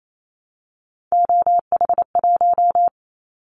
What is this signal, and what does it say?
Morse code, heard as a keyed tone, about 28 wpm, 700 Hz. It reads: OH1
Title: Callsign Prefixes for Top 50 DXCC - 28wpm